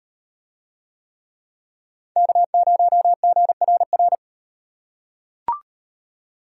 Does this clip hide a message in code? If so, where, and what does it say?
Yes: Morse code from 2.2 to 4.2 s, heard as a keyed tone, about 38 wpm, 700 Hz. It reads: K0GRR